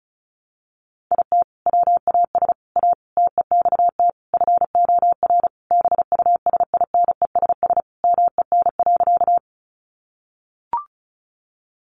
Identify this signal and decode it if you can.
Morse code, heard as a keyed tone, about 34 wpm, 700 Hz. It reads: IT WAS A TEXT FOR BUSINESS MEN.
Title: Sentences from Top 1200 Words 35wpm